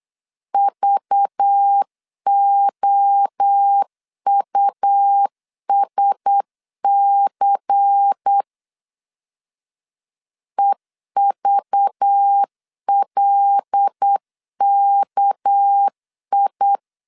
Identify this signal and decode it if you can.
Morse code, heard as a keyed tone, about 8 wpm, 800 Hz. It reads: VOUSC EVLKI
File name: LEZIONI / 7 GF9Z / 7 GF9Z 8wpm 3213